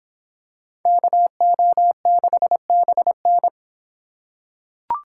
Morse code, heard as a keyed tone, about 25 wpm, 700 Hz. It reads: KO6BN